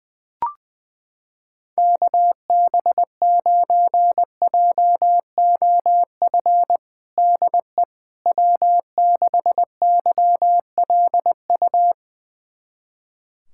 Morse code, heard as a keyed tone, about 21 wpm, 700 Hz. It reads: KB9JOF DE W6YLU
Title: USA Callsign DE Callsign 20wpm